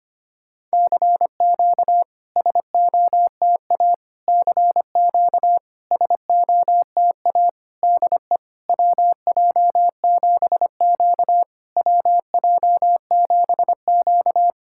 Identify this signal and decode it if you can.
Morse code, heard as a keyed tone, about 25 wpm, 690 Hz. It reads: CQ SOTA CQ SOTA DE WJ7Q WJ7Q
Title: SOTA QSO Practice 25wpm